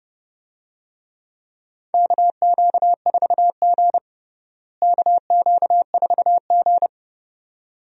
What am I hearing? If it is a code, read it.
Morse code, heard as a keyed tone, about 30 wpm, 700 Hz. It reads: KQ4G KQ4G
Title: USA Call Signs - 4 Characters - Repeated 2x - 30wpm